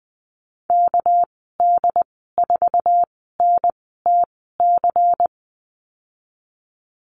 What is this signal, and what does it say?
Morse code, heard as a keyed tone, about 20 wpm, spaced as for 14 wpm, 700 Hz. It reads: KD4NTC
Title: 245 Lesson 39g - Letter z - Call Signs 20wpm